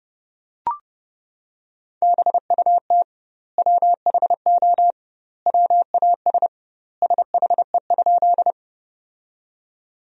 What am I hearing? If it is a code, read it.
Morse code, heard as a keyed tone, about 30 wpm, 700 Hz. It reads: BUT WHO WAS SHE?
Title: 2x Word-Spacing - Sentences from Top 100 Words - 30wpm